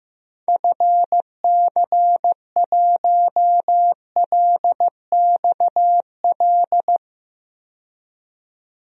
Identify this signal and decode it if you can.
Morse code, heard as a keyed tone, about 15 wpm, 700 Hz. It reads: FC1LXL
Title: DXCC-100 Call Signs - 6 Characters - 15wpm